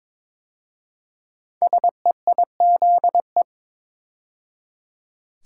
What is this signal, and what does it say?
Morse code, heard as a keyed tone, about 22 wpm, 700 Hz. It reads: SEIZE
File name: 5 Letter Words - First 3 Letters Given - 22wpm